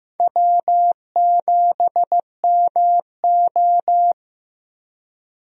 Morse code, heard as a keyed tone, about 15 wpm, 700 Hz. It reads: W7MO
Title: USA Call Signs - 4 Characters - Repeated 3x - 15wpm